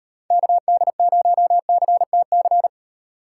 Morse code, heard as a keyed tone, about 30 wpm, 700 Hz. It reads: KD0CTC
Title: Speed Racing - USA 2x3 Call Signs 25wpm